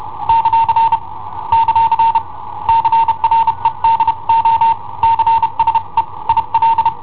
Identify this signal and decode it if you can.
Morse code, heard as a keyed tone, about 31 wpm, 910 Hz. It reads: ; ; CREDO CSE IL
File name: i7ohp / cassetta i7ohp B